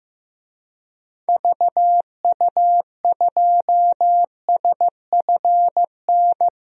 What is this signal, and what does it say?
Morse code, heard as a keyed tone, about 15 wpm, 700 Hz. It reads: VU2SFN
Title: DXCC-100 Call Signs - 6 Characters - 15wpm